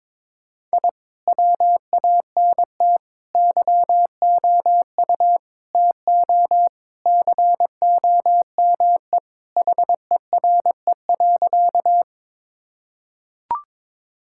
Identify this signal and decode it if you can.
Morse code, heard as a keyed tone, about 22 wpm, 700 Hz. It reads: I WANT YOU TO COME HERE.
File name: practice / 22wpm / Sentences with 2-Word Phrases - 22wpm